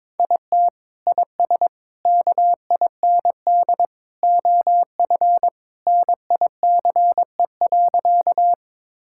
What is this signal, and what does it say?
Morse code, heard as a keyed tone, about 22 wpm, 700 Hz. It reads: IT IS KIND OF NICE.
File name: practice / 22wpm / Sentences with 2-Word Phrases - 22wpm